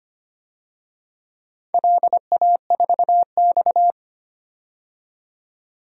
Morse code, heard as a keyed tone, about 25 wpm, 700 Hz. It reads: LA4X
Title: DXCC-100 Call Signs - 4 Characters - 25wpm